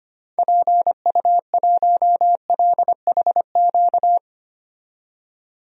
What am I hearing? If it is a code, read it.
Morse code, heard as a keyed tone, about 25 wpm, 700 Hz. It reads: PU1LHQ
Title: DXCC-100 Call Signs - 6 Characters - 25wpm